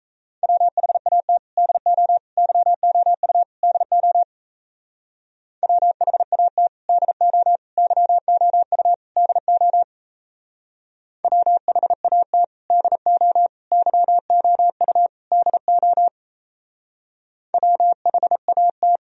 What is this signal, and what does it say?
Morse code, heard as a keyed tone, about 36 wpm, 700 Hz. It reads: WHAT DO YOU DO WHAT DO YOU DO WHAT DO YOU DO WHAT